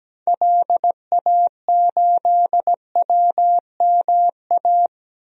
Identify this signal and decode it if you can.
Morse code, heard as a keyed tone, about 17 wpm, 700 Hz. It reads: LA8WMA
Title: DXCC-100 Call Signs - 6 Characters - 17wpm